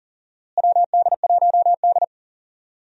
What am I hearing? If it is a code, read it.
Morse code, heard as a keyed tone, about 39 wpm, 700 Hz. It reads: WD1D
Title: USA Call Signs - 4 Characters - Repeated 2x - 40wpm